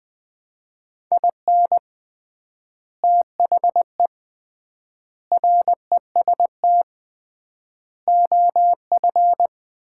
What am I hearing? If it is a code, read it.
Morse code, heard as a keyed tone, about 19 wpm, 700 Hz. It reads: IN THE REST OF
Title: 3x Word-Spacing - Sets of 5 Words - 20wpm